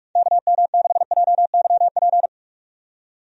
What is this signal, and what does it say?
Morse code, heard as a keyed tone, about 46 wpm, 700 Hz. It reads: KM6JYP
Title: USA Call Signs - 6 Characters - 45wpm